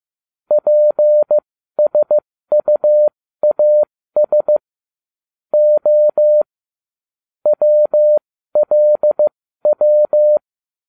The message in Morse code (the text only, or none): PSUAS O WLW